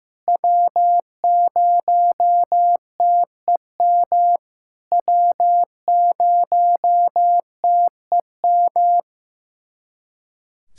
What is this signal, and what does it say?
Morse code, heard as a keyed tone, about 16 wpm, 700 Hz. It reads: W0TEM W0TEM
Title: SOTA QSO Practice 15wpm